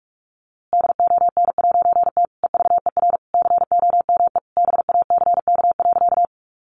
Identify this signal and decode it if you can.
Morse code, heard as a keyed tone, about 44 wpm, 700 Hz. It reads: DON'T EVER COME BACK.
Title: A Sound Of Thunder 45wpm